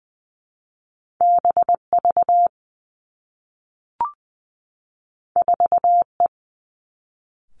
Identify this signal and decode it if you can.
Morse code, heard as a keyed tone, about 20 wpm, 700 Hz. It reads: B V 4 E
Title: Two Challenging Letter-Number Combinations 20wpm